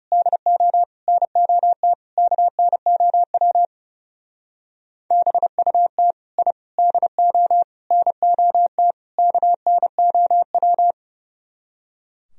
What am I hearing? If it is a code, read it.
Morse code, heard as a keyed tone, about 39 wpm, 700 Hz. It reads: DO NOT KNOW BUT I DO NOT KNOW